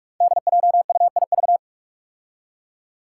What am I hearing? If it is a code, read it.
Morse code, heard as a keyed tone, about 46 wpm, 700 Hz. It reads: DJUIV
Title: Five Challenging Letter-Number Combinations 45wpm